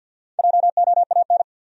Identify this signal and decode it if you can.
Morse code, heard as a keyed tone, about 48 wpm, 700 Hz. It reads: JOAN
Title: USA Common Names - Max 5 Letters - 50wpm